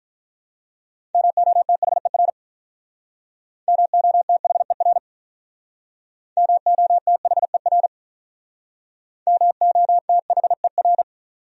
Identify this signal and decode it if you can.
Morse code, heard as a keyed tone, about 45 wpm, 700 Hz. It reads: MOTHER MOTHER MOTHER MOTHER